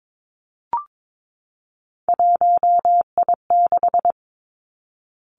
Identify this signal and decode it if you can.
Morse code, heard as a keyed tone, about 22 wpm, 700 Hz. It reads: 1I6
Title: Three Challenging Letter-Number Combinations 22wpm